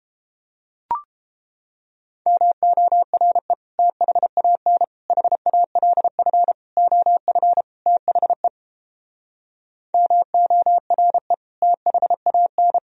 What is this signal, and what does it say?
Morse code, heard as a keyed tone, about 29 wpm, 700 Hz. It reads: MORE THAN HALF OF THE MORE THAN